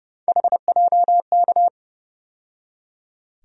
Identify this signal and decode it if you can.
Morse code, heard as a keyed tone, about 30 wpm, 700 Hz. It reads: HJK
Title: USA Call Sign Postfixes - 3 Letters 30wpm